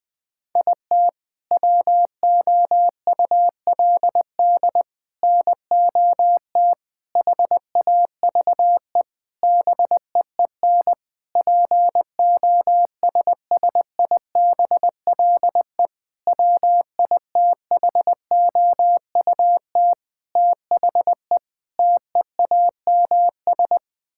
Morse code, heard as a keyed tone, about 20 wpm, 700 Hz. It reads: IT WOULD NOT HAVE BEEN POSSIBLE WITHOUT THE TEAMS
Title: Sentences with 7-Word Phrases - 20wpm